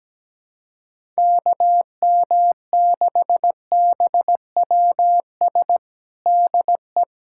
Morse code, heard as a keyed tone, about 17 wpm, 700 Hz. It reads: KM6BWS DE
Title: USA Callsign DE Callsign - 6 Characters - 17wpm